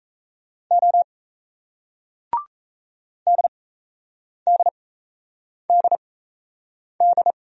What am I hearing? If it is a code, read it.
Morse code, heard as a keyed tone, about 35 wpm, spaced as for 8 wpm, 700 Hz. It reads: O DDDD
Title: Speed-Racing Single Letters 28wpm